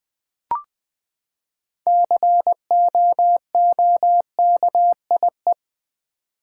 Morse code, heard as a keyed tone, about 20 wpm, 700 Hz. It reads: COOKIE
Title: Six Letter Words - 20wpm